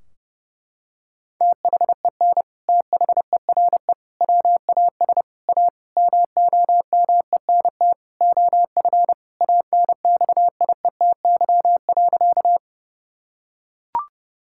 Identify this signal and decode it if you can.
Morse code, heard as a keyed tone, about 32 wpm, 700 Hz. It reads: THEN THERE WAS A MOMENT OF ANXIETY.
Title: Sentences from Top 1600 Words 30wpm